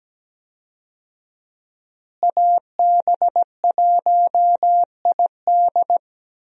Morse code, heard as a keyed tone, about 17 wpm, 700 Hz. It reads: AB1ID
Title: USA Call Signs - 5 Characters - Repeated 3x - 17wpm